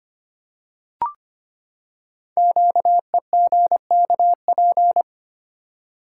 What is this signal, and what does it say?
Morse code, heard as a keyed tone, about 25 wpm, 700 Hz. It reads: QEGKP
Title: Code Groups - 5 Letters - 25wpm